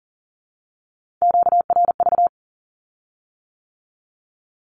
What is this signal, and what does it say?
Morse code, heard as a keyed tone, about 38 wpm, 700 Hz. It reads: QRV
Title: Q-Codes with Spoken Meaning - 40wpm